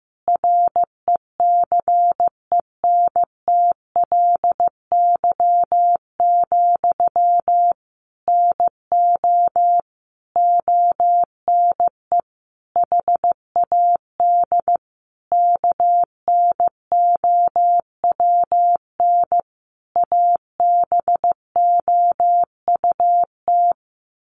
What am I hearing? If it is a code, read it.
Morse code, heard as a keyed tone, about 15 wpm, 700 Hz. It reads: RECENTLY, NO ONE HAD KNOWN ABOUT
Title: Sentences from Top 1200 Words 15wpm